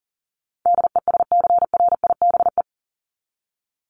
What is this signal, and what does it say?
Morse code, heard as a keyed tone, about 40 wpm, 700 Hz. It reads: DESCRIBE
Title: Top 751-1000 Words 40wpm